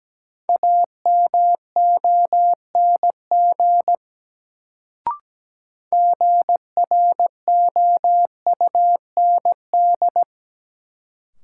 Morse code, heard as a keyed tone, about 17 wpm, 700 Hz. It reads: AMONG GROUND
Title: Sentences from Top 500 Words - New Word Drill - 17wpm